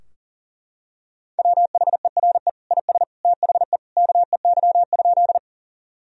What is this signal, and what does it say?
Morse code, heard as a keyed tone, about 40 wpm, 700 Hz. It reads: WHERE IS THE KEY?
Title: Sentences from Top 1000 Words - Review Part 4 - 40wpm